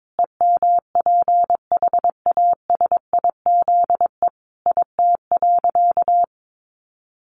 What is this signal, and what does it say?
Morse code, heard as a keyed tone, about 22 wpm, 700 Hz. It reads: EMPHASIZE IT.
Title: Sentences from Top 1500 Words 22wpm